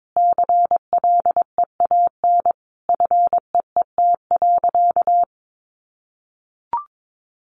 Morse code, heard as a keyed tone, about 22 wpm, 700 Hz. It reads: CLEAN FEET.
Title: Sentences from Top 1100 Words 22wpm